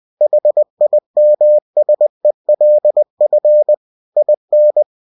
Morse code, heard as a keyed tone, about 20 wpm, 600 Hz. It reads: HIMSELF IN
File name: cw-books / winnie-the-pooh / winnie-20wpm-page-0015